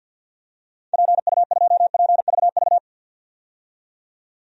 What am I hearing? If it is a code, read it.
Morse code, heard as a keyed tone, about 50 wpm, 700 Hz. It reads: PU2PVV
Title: DXCC-100 Call Signs - 6 Characters - 50wpm